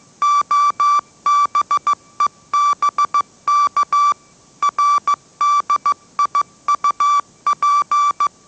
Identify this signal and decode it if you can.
Morse code, heard as a keyed tone, about 15 wpm, 1160 Hz. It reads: OBEBK RDIUP
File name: mrx / mrx065 16wpm